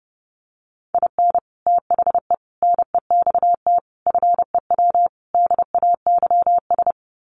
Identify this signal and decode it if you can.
Morse code, heard as a keyed tone, about 30 wpm, 700 Hz. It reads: IN THE NEXT FEW DAYS
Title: Sets of 5 Words Encore2 30wpm